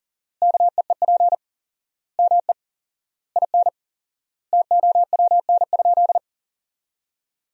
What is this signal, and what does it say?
Morse code, heard as a keyed tone, about 40 wpm, 700 Hz. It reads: KEEP ME IN TOWN?
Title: 4x Word-Spacing - Sentences from Top 500 Words - 40wpm